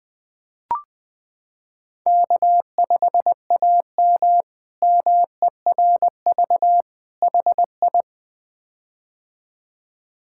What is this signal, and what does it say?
Morse code, heard as a keyed tone, about 20 wpm, 700 Hz. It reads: K5AM MERV HI
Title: K1USN SST QSO Practice - 20wpm